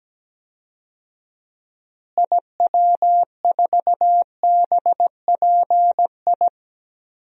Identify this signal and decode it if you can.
Morse code, heard as a keyed tone, about 17 wpm, 700 Hz. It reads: IW4BPI
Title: DXCC-100 Call Signs - 6 Characters - Repeated 2x - 17wpm